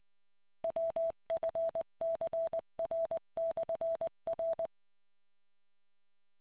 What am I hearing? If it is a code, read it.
Morse code, heard as a keyed tone, about 20 wpm, 660 Hz. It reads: WFCR/R